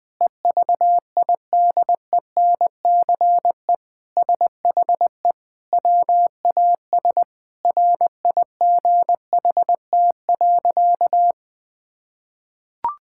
Morse code, heard as a keyed tone, about 20 wpm, 700 Hz. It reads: EVIDENCE SHE WAS RIGHT.
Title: Sentences with 4-Word Phrases - 20wpm